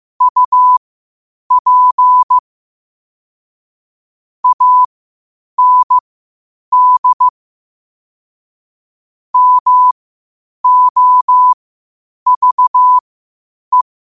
Morse code, heard as a keyed tone, about 15 wpm, spaced as for 8 wpm, 1000 Hz. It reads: UP AND MOVE
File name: qotdfiles / 8wpm